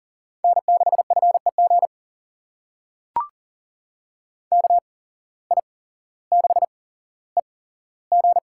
Morse code, heard as a keyed tone, about 40 wpm, spaced as for 17 wpm, 700 Hz. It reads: N6FEG KI6EG
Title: USA Call Signs - 5 Characters - 8x Character Spacing - 40wpm